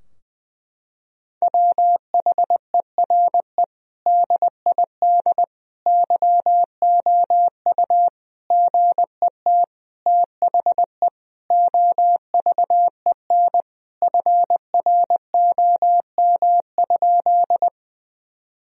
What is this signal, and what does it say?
Morse code, heard as a keyed tone, about 21 wpm, 700 Hz. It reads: WHERE DID YOU GET THE OVEN FROM?
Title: Sentences from Top 1000 Words - Review Part 4 - 20wpm